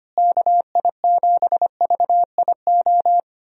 Morse code, heard as a keyed tone, about 25 wpm, 700 Hz. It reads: KI7VIO
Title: USA Call Signs - 6 Characters - Repeated 3x - 25wpm